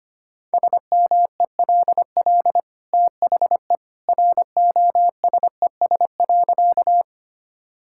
Morse code, heard as a keyed tone, about 25 wpm, 700 Hz. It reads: SMELL THE ROSES.
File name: Easy Sentences with Hints - 25wpm